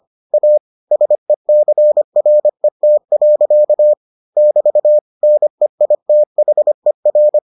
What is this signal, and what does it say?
Morse code, heard as a keyed tone, about 25 wpm, 600 Hz. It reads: A SECRET. <BT> NEITHER